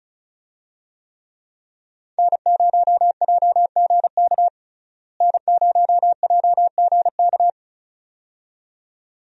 Morse code, heard as a keyed tone, about 36 wpm, 700 Hz. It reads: N0JGK N0JGK
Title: USA Call Signs - 5 Characters - Repeated 2x - 35wpm